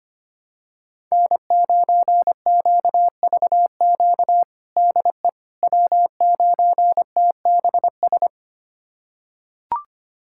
Morse code, heard as a keyed tone, about 25 wpm, 700 Hz. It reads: N9QVQ DE W9TBS
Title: USA Callsign DE Callsign - 5 Characters - 25wpm